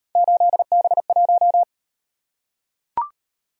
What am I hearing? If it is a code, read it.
Morse code, heard as a keyed tone, about 38 wpm, 700 Hz. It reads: 8B1